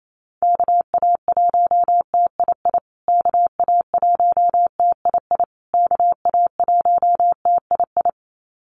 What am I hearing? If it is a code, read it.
Morse code, heard as a keyed tone, about 28 wpm, 700 Hz. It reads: KA1TII KA1TII KA1TII
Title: USA Call Signs Repeated 3x 28wpm